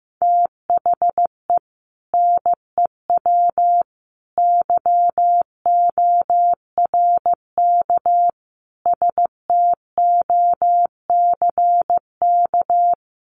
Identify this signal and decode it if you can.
Morse code, heard as a keyed tone, about 15 wpm, 700 Hz. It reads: THE NEW YORK STOCK
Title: Sets of 5 Words 15wpm